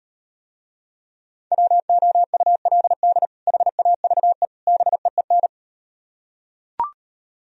Morse code, heard as a keyed tone, about 38 wpm, 700 Hz. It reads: WOULD HAVE BEEN